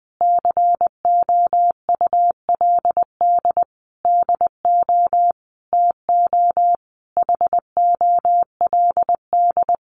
Morse code, heard as a keyed tone, about 19 wpm, 700 Hz. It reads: COULD DO TO HOLD
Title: Sentences from Top 400 Words 20wpm